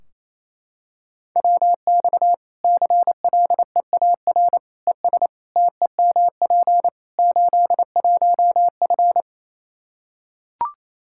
Morse code, heard as a keyed tone, about 27 wpm, 700 Hz. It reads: WX CLEAR ES TEMP 81F
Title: Practice QSO - Exchange 2 - Info - 28wpm